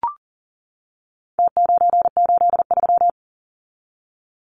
Morse code, heard as a keyed tone, about 40 wpm, 700 Hz. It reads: T983
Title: Numbers - Contest Style - 4-Digit 40wpm